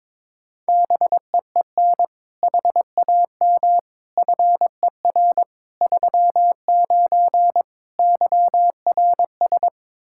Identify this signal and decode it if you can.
Morse code, heard as a keyed tone, about 22 wpm, 700 Hz. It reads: BEEN HAM FER 39 YRS
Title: K1USN SST QSO Practice 22wpm